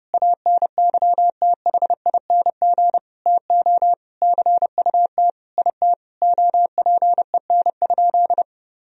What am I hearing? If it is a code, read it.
Morse code, heard as a keyed tone, about 32 wpm, 700 Hz. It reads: ANYTHING TO CUT IT OPEN?
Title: Sentences from Top 1000 Words - Review Part 1 - 30wpm